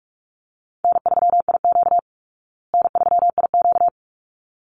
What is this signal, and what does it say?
Morse code, heard as a keyed tone, about 44 wpm, 700 Hz. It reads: N3IQ N3IQ
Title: Easy Call Signs Repeated 2x - 45wpm